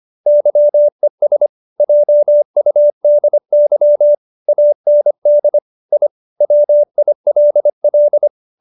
Morse code, heard as a keyed tone, about 25 wpm, 570 Hz. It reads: YES JUDY AND I WILL
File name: QSO312 25